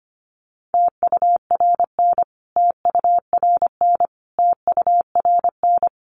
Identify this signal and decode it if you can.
Morse code, heard as a keyed tone, about 25 wpm, 700 Hz. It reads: TURN TURN TURN
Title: Top 200 Words Repeated 25wpm